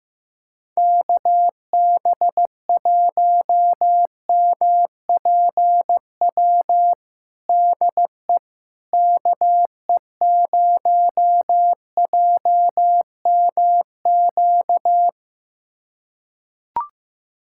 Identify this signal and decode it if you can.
Morse code, heard as a keyed tone, about 15 wpm, 700 Hz. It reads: KB1MPW DE KE0JMQ
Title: USA Callsign DE Callsign - 6 Characters - 15wpm